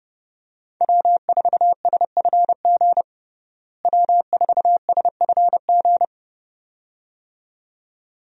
Morse code, heard as a keyed tone, about 30 wpm, 700 Hz. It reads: W4SFG W4SFG
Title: USA Call Signs - 5 Characters - Repeated 2x - 30wpm